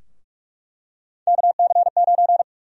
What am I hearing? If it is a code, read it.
Morse code, heard as a keyed tone, about 47 wpm, 700 Hz. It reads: KC9
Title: USA Call Sign Prefixes - Repeated 2x - 45wpm